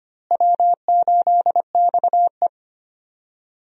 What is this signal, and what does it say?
Morse code, heard as a keyed tone, about 25 wpm, 700 Hz. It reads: W8XE
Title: USA Call Signs - 4 Characters - Repeated 2x - 25wpm